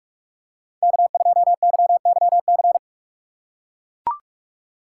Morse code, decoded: K2YYC

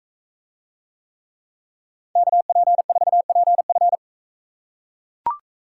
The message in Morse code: KP4PF

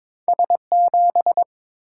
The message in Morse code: S7

